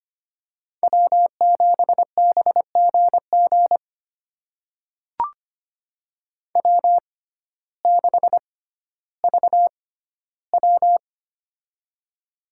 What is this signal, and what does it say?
Morse code, heard as a keyed tone, about 25 wpm, spaced as for 14 wpm, 700 Hz. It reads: W7BGG W6VW
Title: USA Call Signs - 6x Character Spacing - 25wpm